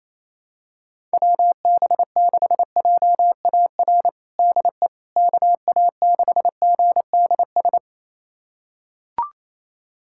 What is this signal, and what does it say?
Morse code, heard as a keyed tone, about 28 wpm, 700 Hz. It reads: WB6JAR DE KA6GDS
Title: USA Callsign DE Callsign - 6 Characters - 28wpm